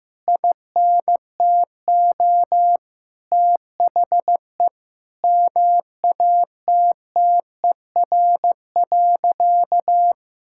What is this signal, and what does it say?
Morse code, heard as a keyed tone, about 15 wpm, 700 Hz. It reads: INTO THE MATTER.
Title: Sentences from Top 1000 Words - Review Part 6 - 15wpm